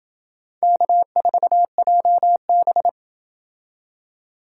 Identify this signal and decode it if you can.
Morse code, heard as a keyed tone, about 28 wpm, 700 Hz. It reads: K4JB